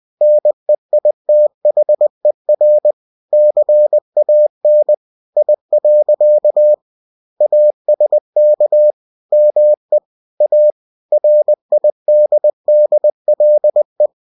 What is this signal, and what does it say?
Morse code, heard as a keyed tone, about 20 wpm, 600 Hz. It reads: NEITHER CAN I. ASK ME A RIDDLE
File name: cw-books / winnie-the-pooh / winnie-20wpm-page-0046